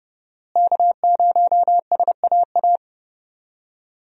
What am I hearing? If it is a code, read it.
Morse code, heard as a keyed tone, about 30 wpm, 700 Hz. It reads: K0SAA
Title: USA Call Signs - 5 Characters - Repeated 3x - 30wpm